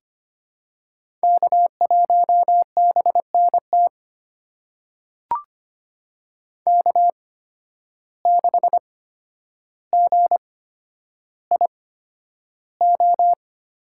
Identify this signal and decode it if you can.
Morse code, heard as a keyed tone, about 25 wpm, spaced as for 11 wpm, 700 Hz. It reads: K1BNT K6GIO
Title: USA Call Signs - 5 Characters - 8x Character Spacing - 25wpm